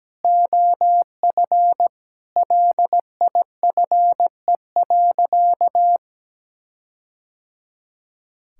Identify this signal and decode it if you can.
Morse code, heard as a keyed tone, about 17 wpm, 700 Hz. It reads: OF LIFE.
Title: Sentences from Top 1600 Words 17wpm